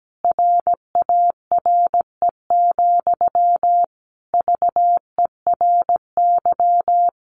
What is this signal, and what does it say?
Morse code, heard as a keyed tone, about 17 wpm, 700 Hz. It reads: RARE, VERY
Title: Sentences from Top 1100 Words 17wpm